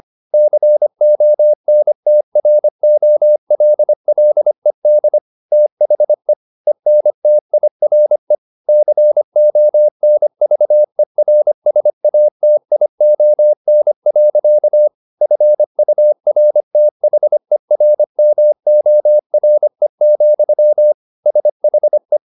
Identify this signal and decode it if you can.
Morse code, heard as a keyed tone, about 25 wpm, 600 Hz. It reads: CONTROLLED THE ENTIRE CONVERSATION. FURTHERMORE, SHE